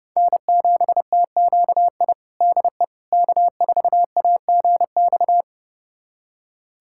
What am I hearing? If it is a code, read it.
Morse code, heard as a keyed tone, about 30 wpm, 700 Hz. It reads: N7TQI DE K4AGX
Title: USA Callsign DE Callsign 30wpm